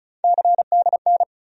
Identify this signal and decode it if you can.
Morse code, heard as a keyed tone, about 36 wpm, 700 Hz. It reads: CDN
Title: USA Call Sign Postfixes - 3 Letters 35wpm